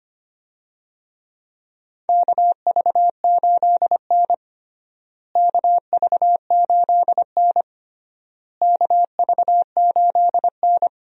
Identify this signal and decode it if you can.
Morse code, heard as a keyed tone, about 25 wpm, 700 Hz. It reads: KV8N KV8N KV8N
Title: USA Call Signs - 4 Characters - Repeated 3x - 25wpm